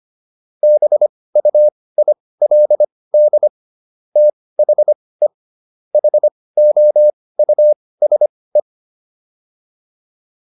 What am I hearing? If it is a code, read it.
Morse code, heard as a keyed tone, about 25 wpm, spaced as for 18 wpm, 600 Hz. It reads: BUILD THE HOUSE